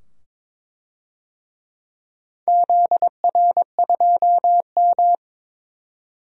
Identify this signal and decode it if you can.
Morse code, heard as a keyed tone, about 22 wpm, 700 Hz. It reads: ZR2M
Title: DXCC-100 Call Signs - 4 Characters - 22wpm